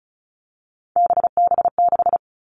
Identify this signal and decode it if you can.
Morse code, heard as a keyed tone, about 34 wpm, 700 Hz. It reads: BB6
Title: All Cut Numbers - 3-Digit 35wpm